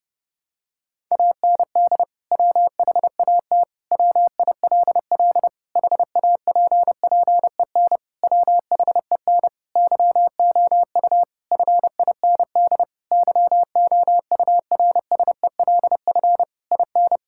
Morse code, heard as a keyed tone, about 30 wpm, 700 Hz. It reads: AND WHAT WILL HAPPEN WHEN YOU FIND YOURSELF IN
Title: Sentences from Top 1000 Words - Review Part 5 - 30wpm